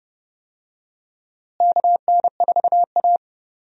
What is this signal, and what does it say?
Morse code, heard as a keyed tone, about 30 wpm, 700 Hz. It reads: KN4A
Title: USA Call Signs - 4 Characters - Repeated 3x - 30wpm